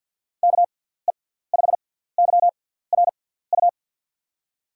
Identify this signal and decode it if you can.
Morse code, heard as a keyed tone, about 52 wpm, spaced as for 17 wpm, 700 Hz. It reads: KE5YRU
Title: USA Call Signs - 6 Characters - 6x Character Spacing - 50wpm